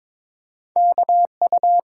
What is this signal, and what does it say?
Morse code, heard as a keyed tone, about 30 wpm, spaced as for 27 wpm, 700 Hz. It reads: KU
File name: Speed-Racing - Two Letter-Number Combinations - 22wpm